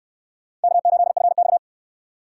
Morse code, heard as a keyed tone, about 59 wpm, 700 Hz. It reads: K8FX